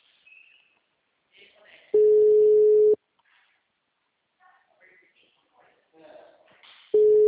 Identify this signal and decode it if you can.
Ringback tone (Europe, 425 Hz), with speech.
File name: out- -2023- - - .9